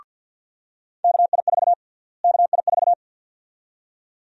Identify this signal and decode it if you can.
Morse code, heard as a keyed tone, about 52 wpm, 700 Hz. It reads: KI4 KI4
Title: USA Call Sign Prefixes - Repeated 2x - 50wpm